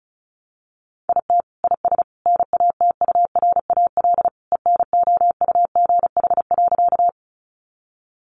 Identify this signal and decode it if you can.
Morse code, heard as a keyed tone, about 34 wpm, 700 Hz. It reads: IT IS NATURAL ENOUGH.
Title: Sentences from Top 700 Words 35wpm